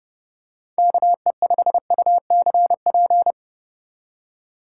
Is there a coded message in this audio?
Yes: Morse code from 0.8 to 3.3 s, heard as a keyed tone, about 30 wpm, 700 Hz. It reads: KE5UCP